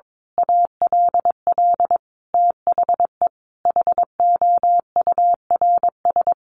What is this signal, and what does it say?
Morse code, heard as a keyed tone, about 22 wpm, 700 Hz. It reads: ALL THE HOURS
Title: A Sound Of Thunder 22wpm